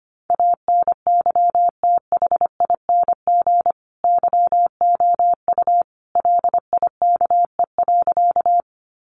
Morse code, heard as a keyed tone, about 25 wpm, 700 Hz. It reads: ANYTHING YOU LIKE.
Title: Sentences from Top 1000 Words 25wpm